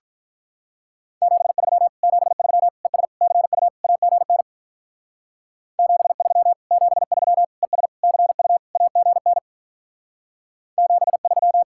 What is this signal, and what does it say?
Morse code, heard as a keyed tone, about 42 wpm, 700 Hz. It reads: 73 73 ES CU AGN 73 73 ES CU AGN 73
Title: Speed-Racing Common QSO Phrases - 35wpm